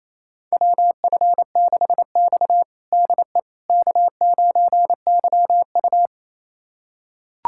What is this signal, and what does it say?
Morse code, heard as a keyed tone, about 28 wpm, 700 Hz. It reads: WF6X DE K9YU
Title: USA Callsign DE Callsign - 4 Characters - 28wpm